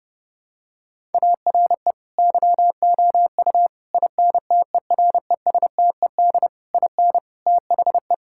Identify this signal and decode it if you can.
Morse code, heard as a keyed tone, about 32 wpm, 700 Hz. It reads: ARE YOU INTERESTED IN THE
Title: Sentences from Top 1000 Words - Review Part 6 - 30wpm